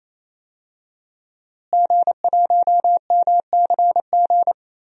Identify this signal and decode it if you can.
Morse code, heard as a keyed tone, about 27 wpm, 700 Hz. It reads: G1MCG
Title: DXCC-100 Call Signs - 5 Characters - Repeated 3x - 28wpm